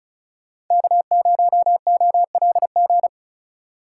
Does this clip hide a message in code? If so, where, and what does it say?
Yes: Morse code from 0.7 to 3.1 s, heard as a keyed tone, about 45 wpm, 700 Hz. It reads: K0OLG